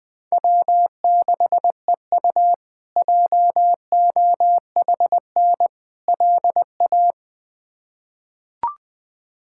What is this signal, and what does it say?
Morse code, heard as a keyed tone, about 20 wpm, 700 Hz. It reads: W6EU JOHN LA
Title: K1USN SST QSO Practice - 20wpm